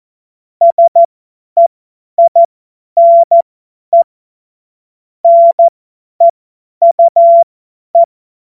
Morse code, heard as a keyed tone, about 14 wpm, spaced as for 10 wpm, 680 Hz. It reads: SEINE NEUE
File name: Morsekurs / Witze / Witz 12 März 2024 50 F70